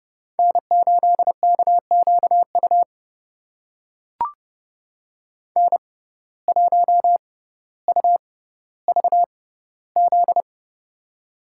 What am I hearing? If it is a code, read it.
Morse code, heard as a keyed tone, about 30 wpm, spaced as for 15 wpm, 700 Hz. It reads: N8KQU N1UVZ
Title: USA Call Signs - 5 Characters - 6x Character Spacing - 30wpm